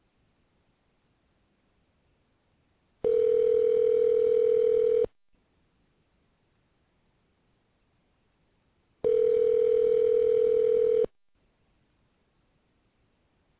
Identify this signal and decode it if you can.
Ringback tone (North America), 440 + 480 Hz.